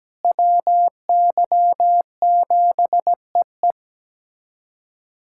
Morse code, heard as a keyed tone, about 17 wpm, 700 Hz. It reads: WY7EE
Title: USA Callsign DE Callsign - 5 Characters - 17wpm